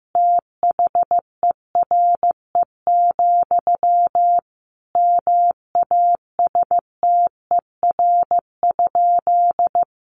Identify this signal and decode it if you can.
Morse code, heard as a keyed tone, about 15 wpm, 700 Hz. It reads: THERE, MASTER?
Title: Sentences from Top 1300 Words 15wpm